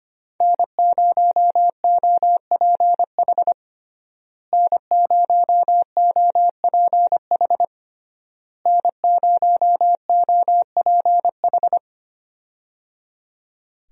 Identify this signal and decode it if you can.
Morse code, heard as a keyed tone, about 25 wpm, 700 Hz. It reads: N0OPH N0OPH N0OPH
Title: USA Call Signs - 5 Characters - Repeated 3x - 25wpm